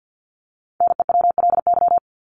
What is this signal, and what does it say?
Morse code, heard as a keyed tone, about 50 wpm, 700 Hz. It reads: NEWLY
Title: LY Suffix - Top 100 Words with Base Words 50wpm